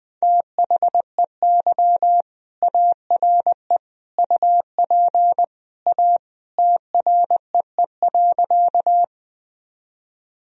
Morse code, heard as a keyed tone, about 20 wpm, 700 Hz. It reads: THEY ARE UP A TREE.
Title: Sentences from Top 1000 Words - Review Part 1 - 20wpm